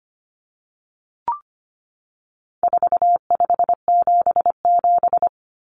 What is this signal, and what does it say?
Morse code, heard as a keyed tone, about 25 wpm, 700 Hz. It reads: 4577
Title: Common Cut Numbers - 4-Digit 25wpm